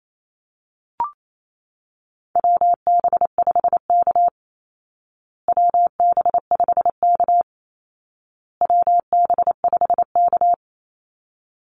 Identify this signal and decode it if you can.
Morse code, heard as a keyed tone, about 28 wpm, 700 Hz. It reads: WB5K WB5K WB5K
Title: Call Signs Repeated 2x - 28wpm